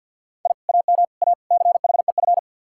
Morse code, heard as a keyed tone, about 64 wpm, 700 Hz. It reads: I AM A CHEF